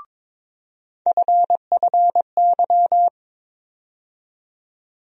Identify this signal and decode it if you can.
Morse code, heard as a keyed tone, about 21 wpm, 700 Hz. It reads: FFY
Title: USA Call Sign Postfixes - 3 Letters 22wpm